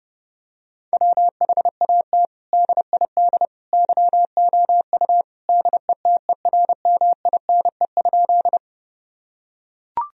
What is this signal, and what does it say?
Morse code, heard as a keyed tone, about 30 wpm, 700 Hz. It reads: WHAT DID YOU DETERMINE?
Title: Sentences from Top 1000 Words - Review Part 4 - 30wpm